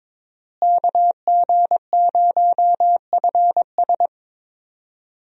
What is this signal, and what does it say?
Morse code, heard as a keyed tone, about 22 wpm, 700 Hz. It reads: KG0FS
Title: USA Call Signs - 5 Characters - Repeated 2x - 22wpm